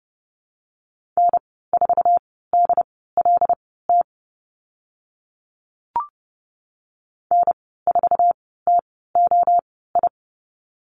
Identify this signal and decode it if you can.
Morse code, heard as a keyed tone, about 30 wpm, spaced as for 17 wpm, 700 Hz. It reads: N4DLT N4TOI
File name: Lesson 15g - Number 5 - Call Signs 30wpm